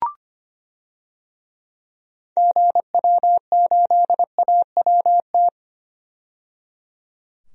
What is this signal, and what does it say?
Morse code, heard as a keyed tone, about 25 wpm, 700 Hz. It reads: GW8AWT
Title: DXCC-100 Call Signs - 6 Characters - 25wpm